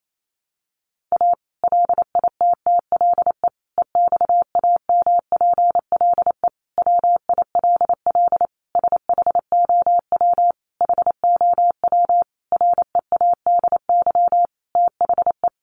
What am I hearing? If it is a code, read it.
Morse code, heard as a keyed tone, about 28 wpm, 700 Hz. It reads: A LITTLE EXAMPLE WILL SHOW HOW READY THE
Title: Sentences from Top 1500 Words 28wpm